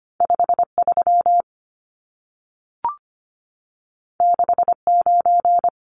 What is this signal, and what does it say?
Morse code, heard as a keyed tone, about 25 wpm, 700 Hz. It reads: 53 69
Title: Numbers - 2-Digit 25wpm